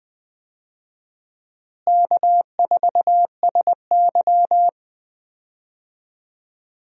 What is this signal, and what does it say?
Morse code, heard as a keyed tone, about 20 wpm, 700 Hz. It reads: K4SY
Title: USA Call Signs - 4 Characters - Repeated 2x - 20wpm